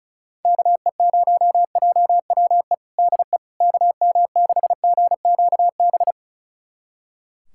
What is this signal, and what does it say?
Morse code, heard as a keyed tone, about 36 wpm, 700 Hz. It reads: KE0JWE DE KM6GQB
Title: USA Callsign DE Callsign 35wpm